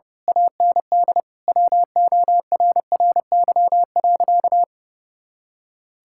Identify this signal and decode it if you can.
Morse code, heard as a keyed tone, about 32 wpm, 700 Hz. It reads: AND WORRY.
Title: Pangram Challenge v2 - 30wpm